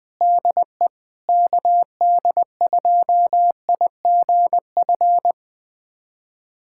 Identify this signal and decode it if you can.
Morse code, heard as a keyed tone, about 20 wpm, 700 Hz. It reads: DE KD2IGF
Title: USA Callsign DE Callsign - 6 Characters - 20wpm